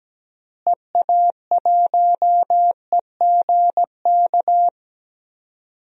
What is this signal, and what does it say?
Morse code, heard as a keyed tone, about 17 wpm, 700 Hz. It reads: EA1EGK
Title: DXCC-100 Call Signs - 6 Characters - Repeated 2x - 17wpm